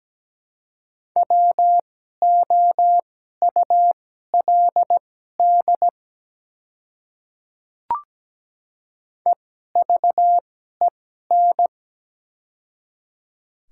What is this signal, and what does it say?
Morse code, heard as a keyed tone, about 17 wpm, spaced as for 12 wpm, 700 Hz. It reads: WOULD EVEN
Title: 2x Character Spacing - Top 300 Words - 17wpm